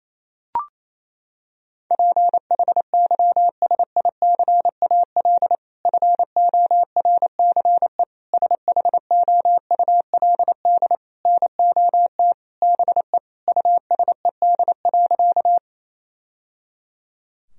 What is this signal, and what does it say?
Morse code, heard as a keyed tone, about 28 wpm, 700 Hz. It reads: PHYSICAL FORCE SHOULD NOT BE USED.
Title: Sentences from Top 1000 Words - Review Part 4 - 28wpm